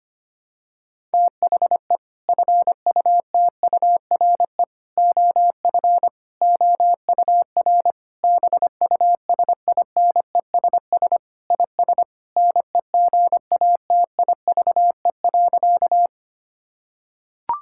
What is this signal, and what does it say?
Morse code, heard as a keyed tone, about 25 wpm, 700 Hz. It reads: THE FUTURE OF OUR BUSINESS IS NEGATIVE.
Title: Sentences from Top 1000 Words - Review Part 6 - 25wpm